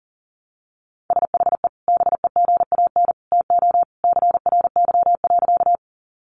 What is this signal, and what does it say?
Morse code, heard as a keyed tone, about 40 wpm, 700 Hz. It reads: SHE BEGAN TO CRY.
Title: Sentences from Top 500 Words 40wpm